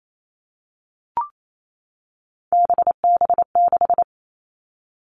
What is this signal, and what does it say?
Morse code, heard as a keyed tone, about 28 wpm, 700 Hz. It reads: BB6
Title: Three Challenging Letter-Number Combinations 28wpm